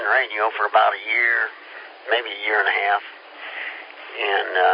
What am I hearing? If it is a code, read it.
Emergency Alert System (SAME) tones, heard as harsh buzzing bursts, with speech. It is heard from 1.0 to 3.9 s.